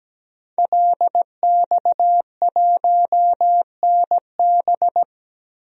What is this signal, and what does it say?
Morse code, heard as a keyed tone, about 17 wpm, 700 Hz. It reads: LX1NB
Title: DXCC-100 Call Signs - 5 Characters - 17wpm